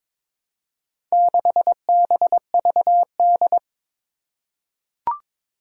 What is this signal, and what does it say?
Morse code, heard as a keyed tone, about 21 wpm, 700 Hz. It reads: 6BVD
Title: Four Challenging Letter-Number Combinations 22wpm